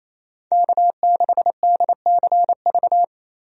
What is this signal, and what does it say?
Morse code, heard as a keyed tone, about 27 wpm, 700 Hz. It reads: K6DCV
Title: USA Call Signs - 5 Characters - 28wpm